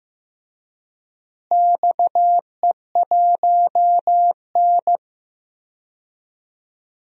Morse code, heard as a keyed tone, about 15 wpm, 700 Hz. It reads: XE1N
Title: DXCC-100 Call Signs - 4 Characters - Repeated 2x - 15wpm